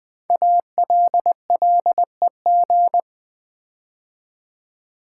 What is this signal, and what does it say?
Morse code, heard as a keyed tone, about 20 wpm, 700 Hz. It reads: ALLEG
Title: Build A Word - 5 Letters At A Time - 10-Letter Words - 20wpm